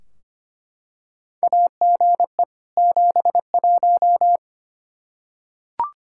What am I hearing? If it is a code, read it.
Morse code, heard as a keyed tone, about 25 wpm, 700 Hz. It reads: AGE 71
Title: Practice QSO - Exchange 3 - More Info - 25wpm